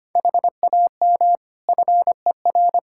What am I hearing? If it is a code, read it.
Morse code, heard as a keyed tone, about 25 wpm, 700 Hz. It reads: HAM FER
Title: Practice QSO - Exchange 3 - More Info - 25wpm